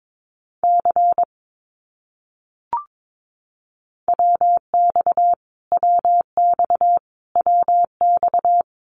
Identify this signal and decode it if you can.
Morse code, heard as a keyed tone, about 22 wpm, 700 Hz. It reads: C WX WX WX
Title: Abbreviations Repeated 3x - 22wpm